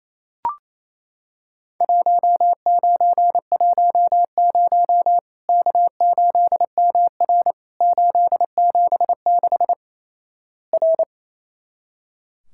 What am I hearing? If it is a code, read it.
Morse code, heard as a keyed tone, about 28 wpm, 700 Hz. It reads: 1910 K8MR 876 R
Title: Worked All Europe DX Contest - QTC Exchange - 28wpm